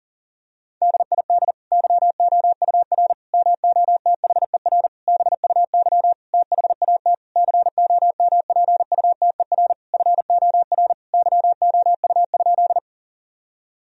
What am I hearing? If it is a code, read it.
Morse code, heard as a keyed tone, about 40 wpm, 700 Hz. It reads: DID YOUR MOTHER BUY THAT COMPUTER FOR YOU?
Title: Sentences from Top 1000 Words - Review Part 3 - 40wpm